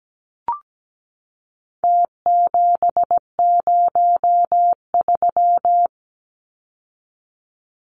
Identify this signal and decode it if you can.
Morse code, heard as a keyed tone, about 17 wpm, 700 Hz. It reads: T703
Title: Numbers - Contest Style - 4-Digit 17wpm